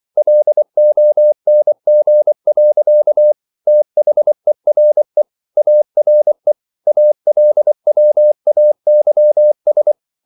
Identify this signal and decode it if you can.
Morse code, heard as a keyed tone, about 24 wpm, 600 Hz. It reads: LONG. THERE ARE ALWAYS